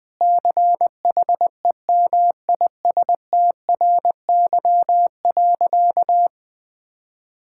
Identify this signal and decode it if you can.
Morse code, heard as a keyed tone, about 21 wpm, 700 Hz. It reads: CHEMISTRY.
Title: Sentences from Top 1000 Words - Review Part 6 - 20wpm